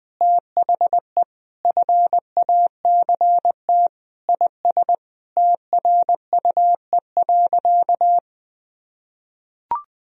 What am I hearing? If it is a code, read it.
Morse code, heard as a keyed tone, about 21 wpm, 700 Hz. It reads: THE FACT IS TRUE.
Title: Sentences from Top 1000 Words - Review Part 3 - 20wpm